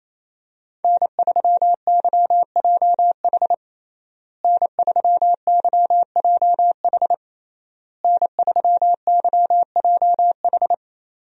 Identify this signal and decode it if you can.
Morse code, heard as a keyed tone, about 28 wpm, 700 Hz. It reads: N3YJH N3YJH N3YJH
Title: USA Call Signs - 5 Characters - Repeated 3x - 28wpm